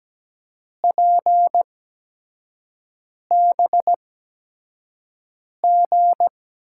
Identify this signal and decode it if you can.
Morse code, heard as a keyed tone, about 17 wpm, spaced as for 5 wpm, 700 Hz. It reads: PBG
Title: USA Call Signs - 8x Character Spacing - 17wpm